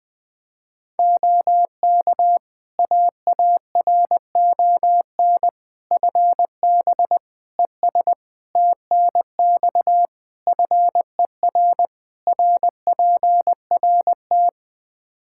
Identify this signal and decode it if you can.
Morse code, heard as a keyed tone, about 20 wpm, 700 Hz. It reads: OK AARON FB ES TNX FER RPRT
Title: Practice QSO - Exchange 2 - Info - 20wpm